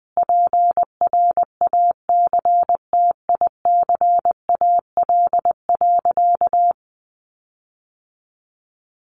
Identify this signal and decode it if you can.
Morse code, heard as a keyed tone, about 20 wpm, 700 Hz. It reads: PRACTICAL.
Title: Sentences from Top 1400 Words 20wpm